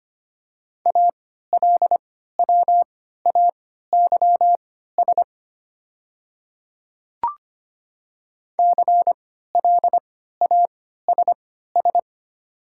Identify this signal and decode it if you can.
Morse code, heard as a keyed tone, about 25 wpm, spaced as for 14 wpm, 700 Hz. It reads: ALWAYS CLASS
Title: 3x Character Spacing - Top 500 Words - 25wpm